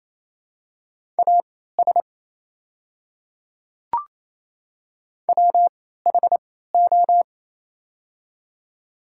Morse code, heard as a keyed tone, about 28 wpm, spaced as for 16 wpm, 700 Hz. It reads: AS WHO
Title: 3x Character Spacing - Top 200 Words - 28wpm